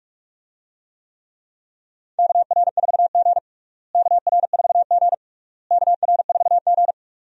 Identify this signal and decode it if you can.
Morse code, heard as a keyed tone, about 47 wpm, 700 Hz. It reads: KR4G KR4G KR4G
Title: USA Call Signs - 4 Characters - Repeated 3x - 45wpm